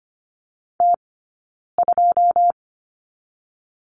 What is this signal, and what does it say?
Morse code, heard as a keyed tone, about 25 wpm, 700 Hz. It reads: T 2
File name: course / 25:10wpm / 092 Lesson 16d - Letter c - Two Characters 25wpm